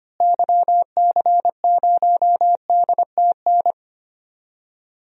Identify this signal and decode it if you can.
Morse code, heard as a keyed tone, about 25 wpm, 700 Hz. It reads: YC0DTN
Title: DXCC-100 Call Signs - 6 Characters - Repeated 2x - 25wpm